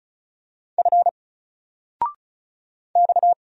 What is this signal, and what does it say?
Morse code, heard as a keyed tone, about 35 wpm, 700 Hz. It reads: F X
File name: Single-Letter-Number-35wpm